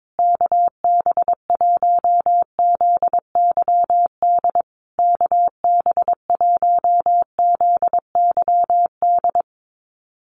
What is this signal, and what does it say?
Morse code, heard as a keyed tone, about 22 wpm, 700 Hz. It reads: KB1ZYD KB1ZYD
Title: USA Call Signs Repeated 3x 22wpm